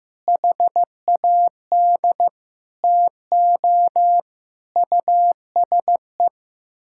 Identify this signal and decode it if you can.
Morse code, heard as a keyed tone, about 15 wpm, 700 Hz. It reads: HAD TO USE
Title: Sentences from Top 1000 Words - Review Part 6 - 15wpm